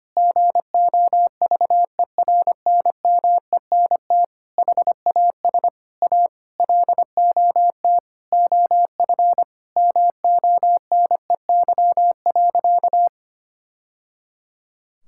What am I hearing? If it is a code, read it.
Morse code, heard as a keyed tone, about 25 wpm, 700 Hz. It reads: GOVERNMENT HAS A LOT OF MONEY.
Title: Sentences from Top 1000 Words - Review Part 5 - 25wpm